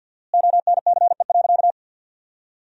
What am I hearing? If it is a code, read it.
Morse code, heard as a keyed tone, about 50 wpm, 700 Hz. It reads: ONCE.
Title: 4x Word-Spacing - Sentences from Top 400 Words - 50wpm